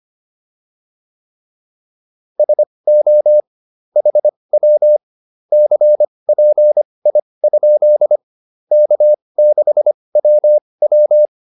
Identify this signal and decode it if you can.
Morse code, heard as a keyed tone, about 25 wpm, spaced as for 20 wpm, 600 Hz. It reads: SO HW CPI? K6WW